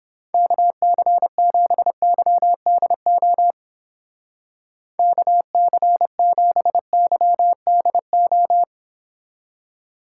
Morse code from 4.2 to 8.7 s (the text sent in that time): KC7YDO